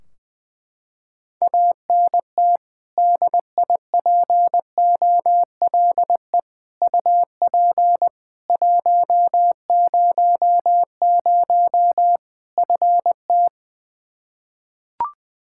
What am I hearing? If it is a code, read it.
Morse code, heard as a keyed tone, about 19 wpm, 700 Hz. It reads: ANT DIPOLE UP 100 FT
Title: Practice QSO - Exchange 2 - Info - 20wpm